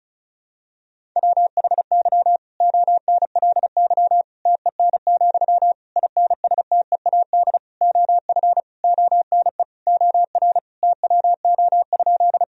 Morse code, heard as a keyed tone, about 34 wpm, 700 Hz. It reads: WHY ONLY TEN, INSTEAD OF ONE OR TWO?
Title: Sentences from Top 1000 Words - Review Part 4 - 35wpm